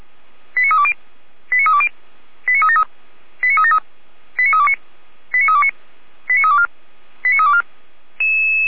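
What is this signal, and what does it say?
Five-tone selective call (ZVEI), heard as quick runs of five tones, tones of about 70 ms. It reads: 89320 89320 89373 89373 89329 89329 89324 89324
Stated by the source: ZVEI five-tone call 89029 received on SDR.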